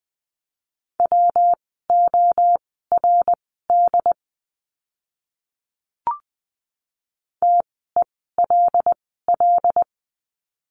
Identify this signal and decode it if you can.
Morse code, heard as a keyed tone, about 20 wpm, spaced as for 14 wpm, 700 Hz. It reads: WORD TELL
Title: 115 Lesson 19f - Letter w - Words 20wpm